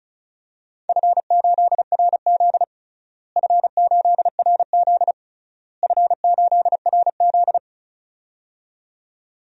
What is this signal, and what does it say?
Morse code, heard as a keyed tone, about 36 wpm, 700 Hz. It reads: F8RZ F8RZ F8RZ
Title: DXCC-100 Call Signs - 4 Characters - Repeated 3x - 35wpm